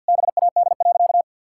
Morse code, heard as a keyed tone, about 50 wpm, 700 Hz. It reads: BAD.
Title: Sentences from Top 1600 Words 50wpm